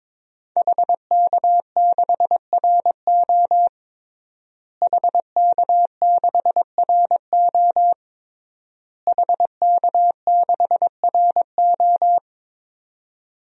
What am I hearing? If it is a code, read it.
Morse code, heard as a keyed tone, about 22 wpm, 700 Hz. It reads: HK6RO HK6RO HK6RO
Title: DXCC-100 Call Signs - 5 Characters - Repeated 3x - 22wpm